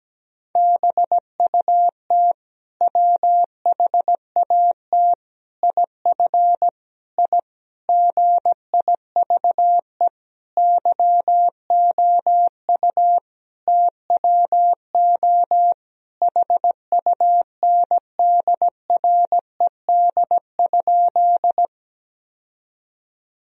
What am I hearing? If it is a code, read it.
Morse code, heard as a keyed tone, about 17 wpm, 700 Hz. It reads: BUT WHAT IF I GIVE YOU TWO HUNDRED?
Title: Sentences from Top 1000 Words - Review Part 2 - 17wpm